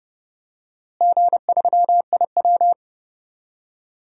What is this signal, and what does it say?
Morse code, heard as a keyed tone, about 30 wpm, 700 Hz. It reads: G3IW
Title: DXCC-100 Call Signs - 4 Characters - Repeated 3x - 30wpm